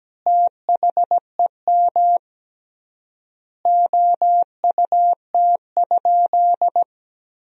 Morse code, heard as a keyed tone, about 17 wpm, 700 Hz. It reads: THEM OUT?
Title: 3x Word-Spacing - Sentences from Top 300 Words - 17wpm